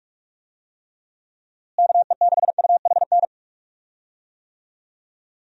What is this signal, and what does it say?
Morse code, heard as a keyed tone, about 46 wpm, 700 Hz. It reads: KE6UHN
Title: USA Call Signs - 6 Characters - Repeated 3x - 45wpm